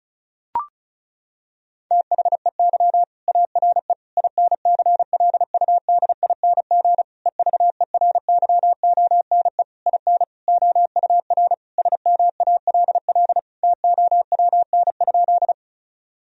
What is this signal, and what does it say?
Morse code, heard as a keyed tone, about 36 wpm, 700 Hz. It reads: THEY ARE INCLUDING EVERYONE IN OUR SMALL TOWN?
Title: Sentences from Top 1000 Words - Review Part 4 - 35wpm